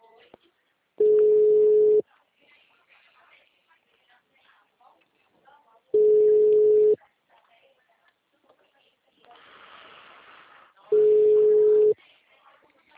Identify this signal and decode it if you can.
Ringback tone (Europe, 425 Hz), with speech.